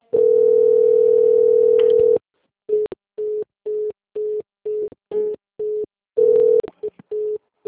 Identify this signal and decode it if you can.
Ringback tone (North America), 440 + 480 Hz.